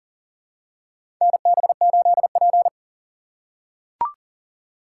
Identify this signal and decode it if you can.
Morse code, heard as a keyed tone, about 40 wpm, 700 Hz. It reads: NB8P
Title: USA Call Signs - 4 Characters - 6x Character Spacing - 40wpm